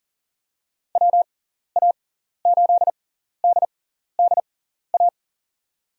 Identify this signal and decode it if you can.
Morse code, heard as a keyed tone, about 40 wpm, spaced as for 14 wpm, 700 Hz. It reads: WA8DDA
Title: USA Call Signs - 6x Character Spacing - 40wpm